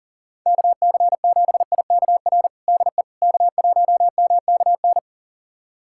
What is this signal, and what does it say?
Morse code, heard as a keyed tone, about 40 wpm, 700 Hz. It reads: KC7IKR DE K1MKN
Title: USA Callsign DE Callsign 40wpm